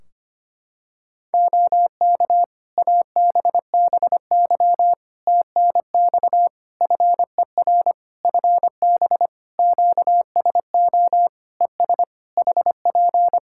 Morse code, heard as a keyed tone, about 25 wpm, 700 Hz. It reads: OK ABBY TNX FER FB QSO ES HP
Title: Practice QSO - Exchange 4 - Ending - 25wpm